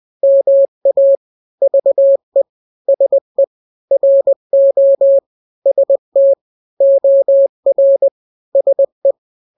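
Morse code, heard as a keyed tone, about 20 wpm, 550 Hz. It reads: MA VE SE RO ST OR SE
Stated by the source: Morse drill of two-letter n-grams at 20/18 WPM (Farnsworth), from KK7RR.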